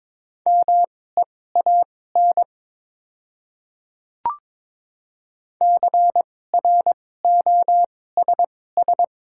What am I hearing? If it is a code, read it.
Morse code, heard as a keyed tone, about 22 wpm, spaced as for 16 wpm, 700 Hz. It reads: MEAN CROSS
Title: 2x Character Spacing - Top 300 Words - 22wpm